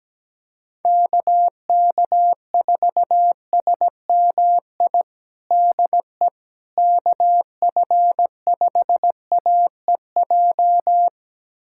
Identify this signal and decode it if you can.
Morse code, heard as a keyed tone, about 17 wpm, 700 Hz. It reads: KK4SMI DE KF5AEJ
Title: USA Callsign DE Callsign - 6 Characters - 17wpm